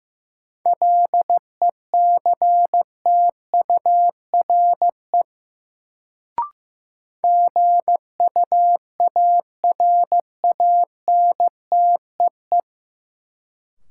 Morse code, heard as a keyed tone, about 15 wpm, 700 Hz. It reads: LECTURE GUARANTEE
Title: Sentences from Top 1600 Words - New Word Drill - 15wpm